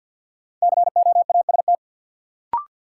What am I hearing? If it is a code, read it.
Morse code, heard as a keyed tone, about 52 wpm, 700 Hz. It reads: COAST
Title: Sentences from Top 1000 Words - Review Part 4 - New Word Drill - 50wpm